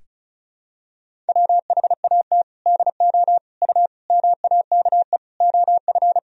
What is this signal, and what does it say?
Morse code, heard as a keyed tone, about 36 wpm, 700 Hz. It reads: WHAT DO U MAKE OF
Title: Sentences from Top 100 Words and Abbreviations 35wpm